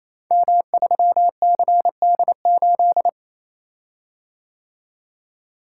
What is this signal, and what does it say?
Morse code, heard as a keyed tone, about 28 wpm, 700 Hz. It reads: M3CD8
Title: Code Groups - 5 Letter:Numbers - 28wpm